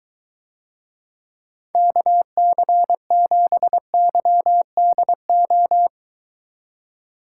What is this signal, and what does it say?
Morse code, heard as a keyed tone, about 23 wpm, 700 Hz. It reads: KC7YDO